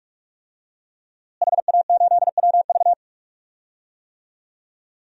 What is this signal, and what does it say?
Morse code, heard as a keyed tone, about 46 wpm, 700 Hz. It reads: HA8WV
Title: DXCC-100 Call Signs - 5 Characters - Repeated 2x - 45wpm